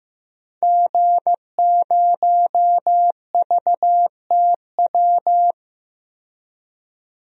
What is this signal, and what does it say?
Morse code, heard as a keyed tone, about 16 wpm, 700 Hz. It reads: G0VTW
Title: DXCC-100 Call Signs - 5 Characters - 15wpm